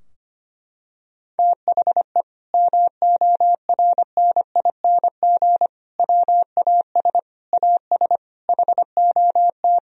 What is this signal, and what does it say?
Morse code, heard as a keyed tone, about 25 wpm, 700 Hz. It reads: THE MORNING WAS AS HOT
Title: Sentences from Top 1000 Words - Review Part 3 - 25wpm